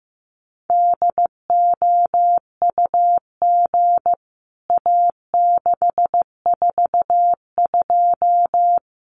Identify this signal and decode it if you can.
Morse code, heard as a keyed tone, about 14 wpm, 700 Hz. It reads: DOUG A642
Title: CWT Exchange 15wpm